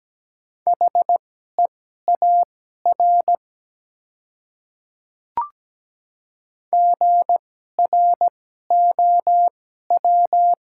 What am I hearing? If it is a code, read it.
Morse code, heard as a keyed tone, about 17 wpm, spaced as for 12 wpm, 690 Hz. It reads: HEAR GROW
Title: 2x Character Spacing - Top 300 Words - 17wpm